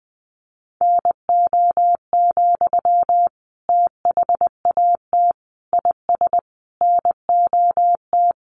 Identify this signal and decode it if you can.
Morse code, heard as a keyed tone, about 19 wpm, 700 Hz. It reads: NO, THAT IS NOT
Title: Sentences from Top 600 Words 20wpm